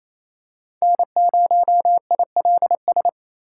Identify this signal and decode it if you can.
Morse code, heard as a keyed tone, about 28 wpm, 700 Hz. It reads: N0ILS
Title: USA Call Signs - 4x Character Spacing - 28wpm